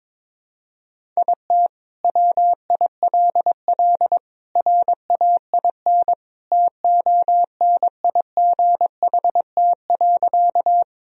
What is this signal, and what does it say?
Morse code, heard as a keyed tone, about 22 wpm, 700 Hz. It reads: IT WILL RAIN TONIGHT.
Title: Sentences from Top 1000 Words - Review Part 1 - 22wpm